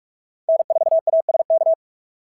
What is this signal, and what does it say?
Morse code, heard as a keyed tone, about 46 wpm, 690 Hz. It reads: N4ASK
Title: SOTA QSO Practice 45wpm